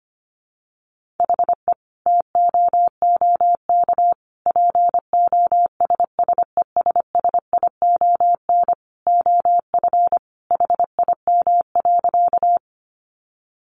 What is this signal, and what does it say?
Morse code, heard as a keyed tone, about 25 wpm, 700 Hz. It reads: HE TOOK POSSESSION OF HIM.
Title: Sentences from Top 1400 Words 25wpm